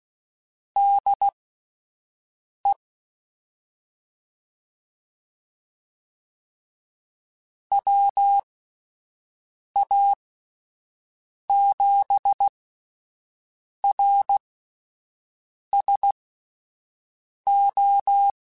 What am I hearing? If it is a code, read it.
Morse code, heard as a keyed tone, about 16 wpm, spaced as for 5 wpm, 800 Hz. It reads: DE WA7RSO
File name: (spaced) / QSOs Spaced-mp3s / WD0FBT 1 @10 sp